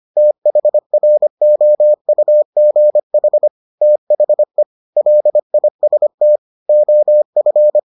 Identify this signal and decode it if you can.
Morse code, heard as a keyed tone, about 25 wpm, 600 Hz. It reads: THROUGH THE LIST OF